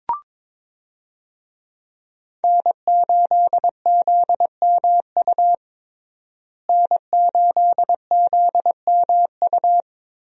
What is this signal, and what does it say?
Morse code, heard as a keyed tone, about 22 wpm, 700 Hz. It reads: N8ZMU N8ZMU
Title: USA Call Signs - 5 Characters - Repeated 3x - 22wpm